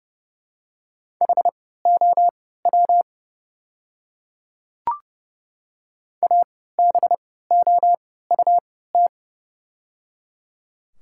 Morse code, heard as a keyed tone, about 30 wpm, spaced as for 17 wpm, 700 Hz. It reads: HOW ABOUT
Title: 3x Character Spacing - Top 100 Words - 30wpm